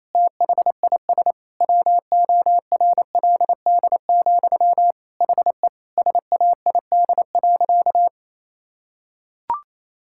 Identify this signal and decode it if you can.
Morse code, heard as a keyed tone, about 27 wpm, 700 Hz. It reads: THIS WORLD, HE SAID.
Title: Sentences from Top 1600 Words 28wpm